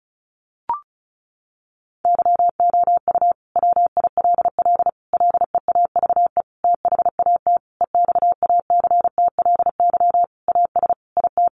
Morse code, heard as a keyed tone, about 34 wpm, 700 Hz. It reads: YOU WILL LEAVE THAT EXACTLY AS IT
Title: Sentences from Top 900 Words 35wpm